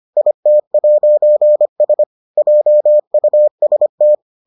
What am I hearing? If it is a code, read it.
Morse code, heard as a keyed tone, about 25 wpm, 600 Hz. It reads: IT'S JUST